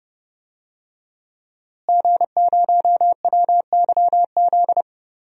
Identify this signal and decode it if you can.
Morse code, heard as a keyed tone, about 31 wpm, 700 Hz. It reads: G0WYZ
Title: DXCC-100 Call Signs - 5 Characters - Repeated 3x - 30wpm